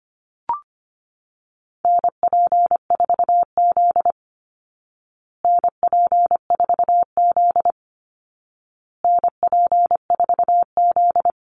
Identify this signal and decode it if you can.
Morse code, heard as a keyed tone, about 25 wpm, 700 Hz. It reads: NP4Z NP4Z NP4Z
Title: Call Signs Repeated 2x - 25wpm